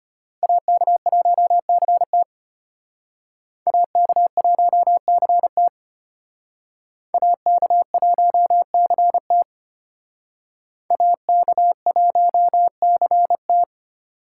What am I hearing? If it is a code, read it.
Morse code, heard as a keyed tone, about 32 wpm, 700 Hz. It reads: AK1CT AK1CT AK1CT AK1CT